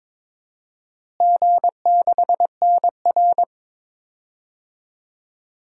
Morse code, heard as a keyed tone, about 21 wpm, 700 Hz. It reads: G6NR
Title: DXCC-100 Call Signs - 4 Characters - Repeated 3x - 22wpm